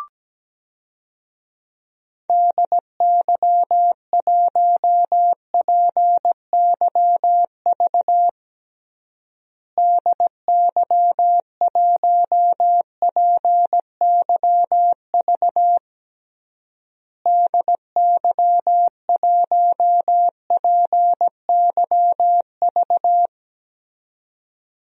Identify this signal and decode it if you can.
Morse code, heard as a keyed tone, about 17 wpm, 700 Hz. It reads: DY1PYV DY1PYV DY1PYV
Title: DXCC-100 Call Signs - 6 Characters - Repeated 3x - 17wpm